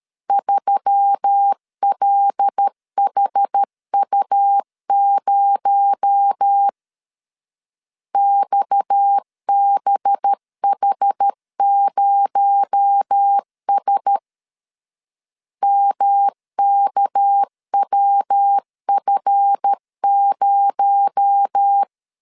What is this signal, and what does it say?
Morse code, heard as a keyed tone, about 13 wpm, 800 Hz. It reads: 3LHU0 XBH0S MKWF0
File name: LEZIONI / 7 GF9Z / 7 GF9 12wpm 2319